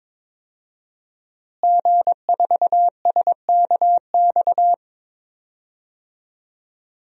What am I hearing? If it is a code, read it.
Morse code, heard as a keyed tone, about 22 wpm, 700 Hz. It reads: G4SKX
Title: DXCC-100 Call Signs - 5 Characters - Repeated 2x - 22wpm